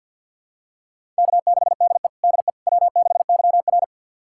Morse code, heard as a keyed tone, about 48 wpm, 700 Hz. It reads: K6DE DE W6YR
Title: USA Callsign DE Callsign - 4 Characters - 50wpm